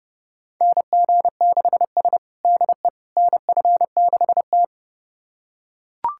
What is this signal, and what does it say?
Morse code, heard as a keyed tone, about 30 wpm, 700 Hz. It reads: NG6S DE NF6T
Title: USA Callsign DE Callsign - 4 Characters - 30wpm